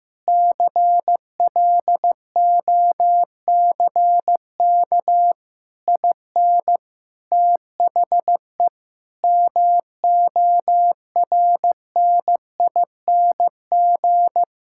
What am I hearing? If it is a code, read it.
Morse code, heard as a keyed tone, about 15 wpm, 700 Hz. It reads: CLOCK IN THE MORNING
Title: Pangram Challenge v2 - 15wpm